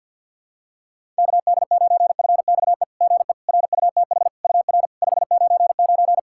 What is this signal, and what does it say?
Morse code, heard as a keyed tone, about 46 wpm, 690 Hz. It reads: KD9FXE GE RUTH UR 599
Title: SOTA QSO Practice 50wpm